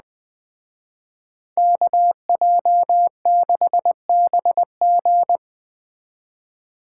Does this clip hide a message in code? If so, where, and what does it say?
Yes: Morse code from 1.6 to 5.4 s, heard as a keyed tone, about 19 wpm, 700 Hz. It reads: KJ6BG